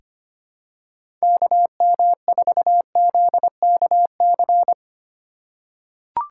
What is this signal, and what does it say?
Morse code, heard as a keyed tone, about 25 wpm, 700 Hz. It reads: KM4ZKC
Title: USA Call Signs - 6 Characters - 3x Character Spacing - 25wpm